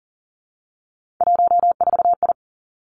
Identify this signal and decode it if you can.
Morse code, heard as a keyed tone, about 42 wpm, 700 Hz. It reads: 14I
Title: Three Challenging Letter-Number Combinations 40wpm